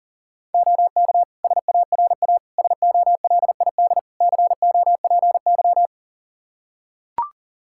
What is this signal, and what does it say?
Morse code, heard as a keyed tone, about 40 wpm, 700 Hz. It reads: OK SARA SOLID COPY
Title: Practice QSO - Exchange 3 - More Info - 40wpm